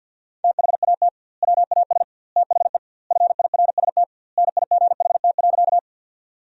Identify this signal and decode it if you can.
Morse code, heard as a keyed tone, about 50 wpm, 700 Hz. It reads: THAT WAS THE FIRST NIGHT.
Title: 2x Word-Spacing - Sentences from Top 300 Words - 50wpm